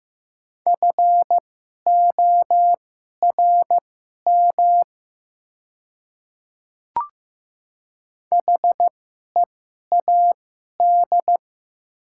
Morse code, heard as a keyed tone, about 15 wpm, spaced as for 11 wpm, 690 Hz. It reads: FORM HEAD
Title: 2x Character Spacing - Top 300 Words - 15wpm